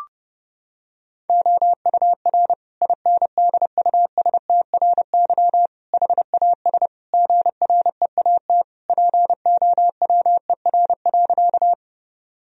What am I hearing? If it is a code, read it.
Morse code, heard as a keyed tone, about 32 wpm, 700 Hz. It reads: OUR INDUSTRY HAS GREAT POWER.
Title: Sentences from Top 1000 Words - Review Part 4 - 30wpm